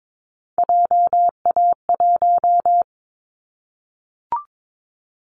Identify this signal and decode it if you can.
Morse code, heard as a keyed tone, about 22 wpm, 700 Hz. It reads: JA1
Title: Callsign Prefixes for Top 25 DXCC - 22wpm